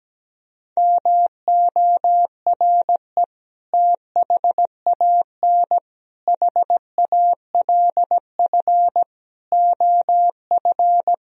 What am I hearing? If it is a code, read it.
Morse code, heard as a keyed tone, about 17 wpm, 700 Hz. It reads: MORE THAN HALF OF
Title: Speed-Racing - Sets of 5 Words 17wpm